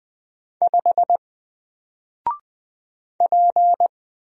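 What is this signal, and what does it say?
Morse code, heard as a keyed tone, about 20 wpm, 700 Hz. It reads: 5 P
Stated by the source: Single letter and number Morse drill at 20 wpm, from N4LCD.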